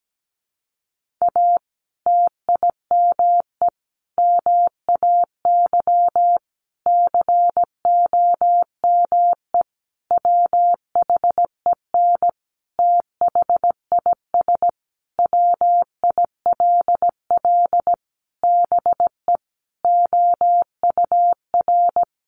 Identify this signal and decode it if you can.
Morse code, heard as a keyed tone, about 17 wpm, 700 Hz. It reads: A TIME MAY COME WHEN THIS WILL BE OUR
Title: Sentences from Top 1500 Words 17wpm